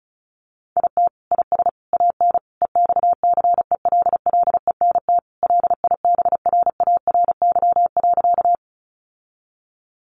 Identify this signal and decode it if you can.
Morse code, heard as a keyed tone, about 34 wpm, 700 Hz. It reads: IT IS AN EXCELLENT LIBRARY.
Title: Sentences from Top 1000 Words 35wpm